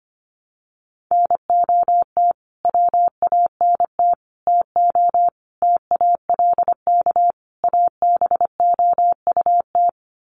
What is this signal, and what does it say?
Morse code, heard as a keyed tone, about 25 wpm, 700 Hz. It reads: NOT WANT TO TALK ABOUT
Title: Sets of 5 Words 25wpm